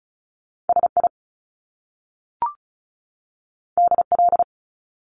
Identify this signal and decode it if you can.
Morse code, heard as a keyed tone, about 36 wpm, 700 Hz. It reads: S I D L
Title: Top Two Letter Combinations in Words 35wpm